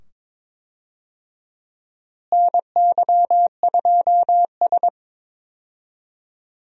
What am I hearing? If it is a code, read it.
Morse code, heard as a keyed tone, about 22 wpm, 700 Hz. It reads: NY2S
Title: USA Call Signs - 4 Characters - Repeated 2x - 22wpm